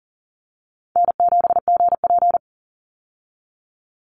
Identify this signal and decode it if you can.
Morse code, heard as a keyed tone, about 40 wpm, 700 Hz. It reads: N7GP
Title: USA Call Signs 40wpm